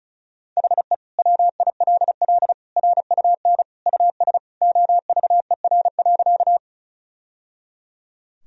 Morse code, heard as a keyed tone, about 34 wpm, 700 Hz. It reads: HE WILL RUN US OVER.
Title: Sentences from Top 1000 Words - Review Part 1 - 35wpm